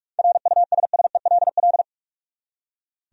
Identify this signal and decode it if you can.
Morse code, heard as a keyed tone, about 46 wpm, 700 Hz. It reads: RUSSELL
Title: USA Common Names - 45wpm